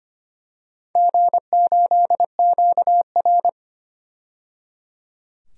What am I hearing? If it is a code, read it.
Morse code, heard as a keyed tone, about 25 wpm, 700 Hz. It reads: G8QR
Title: DXCC-100 Call Signs - 4 Characters - 25wpm